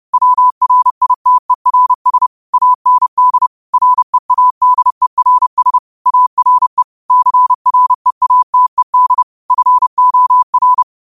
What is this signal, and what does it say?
Morse code, heard as a keyed tone, about 30 wpm, 1000 Hz. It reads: WRITERS AND READERS ARE CREATED FOR
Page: Quotes for Mon, 28 Apr 2025 in Morse Code at 30 words per minute.